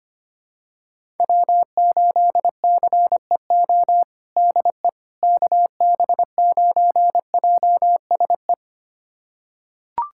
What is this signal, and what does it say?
Morse code, heard as a keyed tone, about 25 wpm, 700 Hz. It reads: W8CEO DE KB9JSE
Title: USA Callsign DE Callsign 25wpm